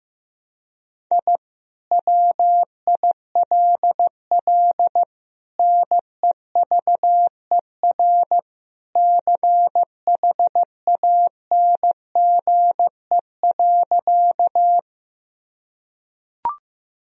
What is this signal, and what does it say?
Morse code, heard as a keyed tone, about 15 wpm, 700 Hz. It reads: I WILL NEVER CHANGE.
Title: Sentences from Top 1000 Words - Review Part 1 - 15wpm